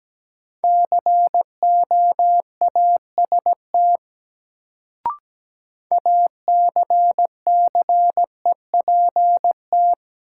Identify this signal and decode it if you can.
Morse code, heard as a keyed tone, about 17 wpm, 700 Hz. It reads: COAST ACCEPT
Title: Sentences from Top 1300 Words - New Word Drill Alt - 17wpm